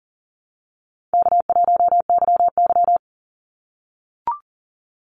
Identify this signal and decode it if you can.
Morse code, heard as a keyed tone, about 40 wpm, 700 Hz. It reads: K1YY
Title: Call Signs - 40wpm